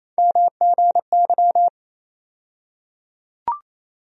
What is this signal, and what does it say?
Morse code, heard as a keyed tone, about 28 wpm, 700 Hz. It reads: MGY
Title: USA Call Sign Postfixes - 3 Letters 28wpm